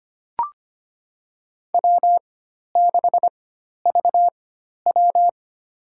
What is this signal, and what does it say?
Morse code, heard as a keyed tone, about 25 wpm, spaced as for 12 wpm, 700 Hz. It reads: W6VW
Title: USA Call Signs - 4x Character Spacing - 25wpm